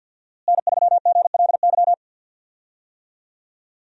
Morse code, heard as a keyed tone, about 52 wpm, 700 Hz. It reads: N3GLY
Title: USA Call Signs - 5 Characters - Repeated 3x - 50wpm